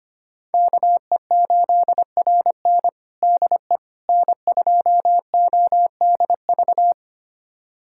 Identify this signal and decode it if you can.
Morse code, heard as a keyed tone, about 25 wpm, 700 Hz. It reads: KE8RN DE N2ODV
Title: USA Callsign DE Callsign - 5 Characters - 25wpm